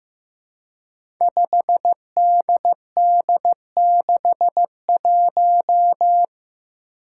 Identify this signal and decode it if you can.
Morse code, heard as a keyed tone, about 15 wpm, 700 Hz. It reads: 5DD61
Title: Five Challenging Letter-Number Combinations 15wpm